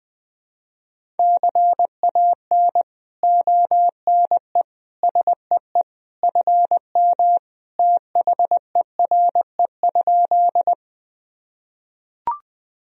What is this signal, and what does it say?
Morse code, heard as a keyed tone, about 20 wpm, 700 Hz. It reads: CAN ONE SEE FM THERE?
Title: Sentences from Top 100 Words and Abbreviations 20wpm